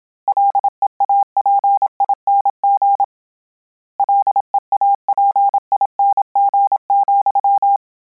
Morse code, heard as a keyed tone, about 27 wpm, 800 Hz. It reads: LEAPING LEAPING,
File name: Combinations / Combinations-mp3s / Combinations@25